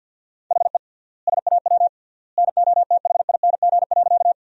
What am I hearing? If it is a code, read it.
Morse code, heard as a keyed tone, about 48 wpm, 700 Hz. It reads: HE SAW NOTHING.
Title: 3x Word-Spacing - Sentences from Top 500 Words - 50wpm